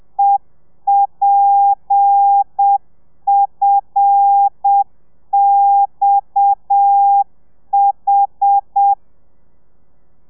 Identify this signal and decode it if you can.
Morse code, heard as a keyed tone, about 7 wpm, 800 Hz. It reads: EPFXH
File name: telegrafia / i2tfj / Lez7 30